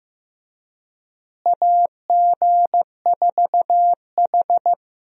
Morse code, heard as a keyed tone, about 15 wpm, 700 Hz. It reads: AG4H
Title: USA Call Signs - 4 Characters - Repeated 2x - 15wpm